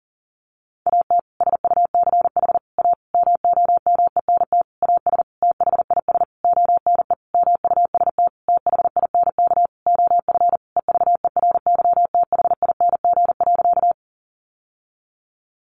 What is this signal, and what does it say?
Morse code, heard as a keyed tone, about 40 wpm, 700 Hz. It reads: AT SUCH A MOMENT AS THIS ONE MUST THINK OF EVERYTHING.
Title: Sentences from Top 1000 Words 40wpm